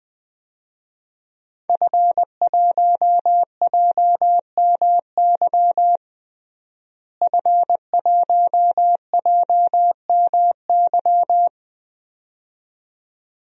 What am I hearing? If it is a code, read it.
Morse code, heard as a keyed tone, about 21 wpm, 700 Hz. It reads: F1JMY F1JMY
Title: DXCC-100 Call Signs - 5 Characters - Repeated 2x - 20wpm